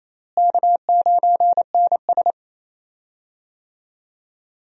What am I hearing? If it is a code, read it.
Morse code, heard as a keyed tone, about 28 wpm, 700 Hz. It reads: K9NS
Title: USA Call Signs - 4 Characters - Repeated 3x - 28wpm